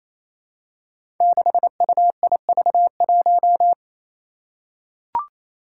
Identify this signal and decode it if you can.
Morse code, heard as a keyed tone, about 27 wpm, 700 Hz. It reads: 6UIV1
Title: Code Groups - 5 Letter:Numbers - 28wpm